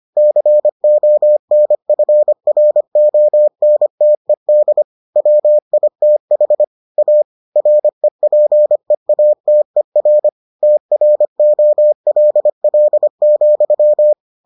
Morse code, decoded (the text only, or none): CONFRONTED WITH A REPEATER TROLL,